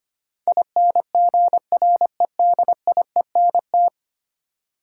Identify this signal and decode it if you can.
Morse code, heard as a keyed tone, about 25 wpm, 700 Hz. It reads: INGREDIENT
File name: Ten Letter Words - 25wpm